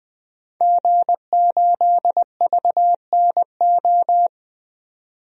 Morse code, heard as a keyed tone, about 20 wpm, 700 Hz. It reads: G8VNO
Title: DXCC-100 Call Signs - 5 Characters - 20wpm